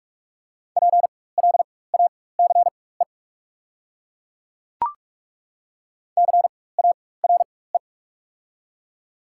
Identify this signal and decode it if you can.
Morse code, heard as a keyed tone, about 42 wpm, spaced as for 20 wpm, 690 Hz. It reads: PLACE CARE
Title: 4x Character Spacing - Top 500 Words - 45wpm